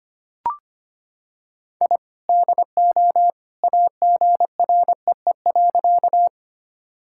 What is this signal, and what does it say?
Morse code, heard as a keyed tone, about 25 wpm, 700 Hz. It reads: I DO AGREE.
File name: practice / 25wpm / Sentences with 2-Word Phrases - 25wpm